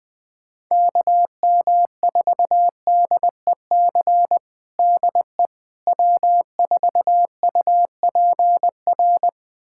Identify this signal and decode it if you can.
Morse code, heard as a keyed tone, about 21 wpm, 700 Hz. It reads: KM4DEC DE W4UPR
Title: USA Callsign DE Callsign 20wpm